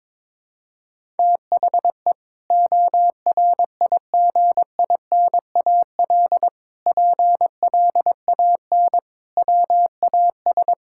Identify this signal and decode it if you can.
Morse code, heard as a keyed tone, about 21 wpm, 700 Hz. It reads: THE ORIGINAL PLAN WAS
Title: Sentences from Top 1000 Words - Review Part 6 - 22wpm